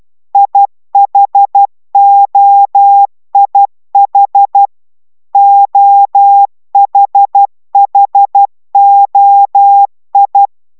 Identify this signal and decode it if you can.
Morse code, heard as a keyed tone, about 12 wpm, 800 Hz. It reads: IHOIH OHHOI